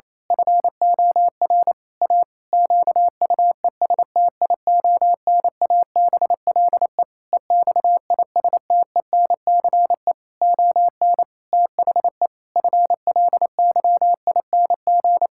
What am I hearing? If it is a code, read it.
Morse code, heard as a keyed tone, about 28 wpm, 700 Hz. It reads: FOR A QUESTIONABLE EXISTENCE ON THE FLYING
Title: Pangram Challenge v2 - 28wpm